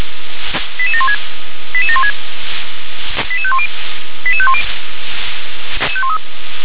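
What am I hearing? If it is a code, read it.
Five-tone selective call (ZVEI), heard as quick runs of five tones, tones of about 70 ms. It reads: 88716 88716 88510 88510
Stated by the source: Five-tone ZVEI selective call received on SDR.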